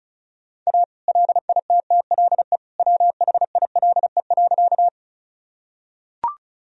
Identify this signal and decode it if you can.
Morse code, heard as a keyed tone, about 36 wpm, 700 Hz. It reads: A LITTLE WHILE.
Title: Sentences from Top 1000 Words - Review Part 1 - 35wpm